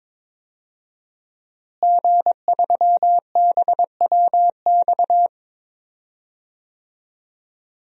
Morse code, heard as a keyed tone, about 22 wpm, 700 Hz. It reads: G3BWX
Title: DXCC-100 Call Signs - 5 Characters - Repeated 3x - 22wpm